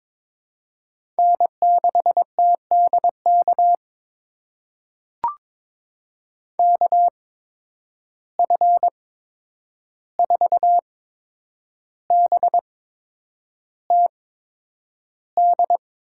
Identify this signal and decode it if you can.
Morse code, heard as a keyed tone, about 22 wpm, spaced as for 9 wpm, 700 Hz. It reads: N6TDK KF4BTD
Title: USA Call Signs - 8x Character Spacing - 22wpm